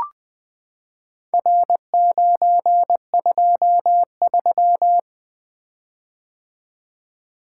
Morse code, heard as a keyed tone, about 20 wpm, 700 Hz. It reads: R923
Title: Four Letter-Number Combinations - 20wpm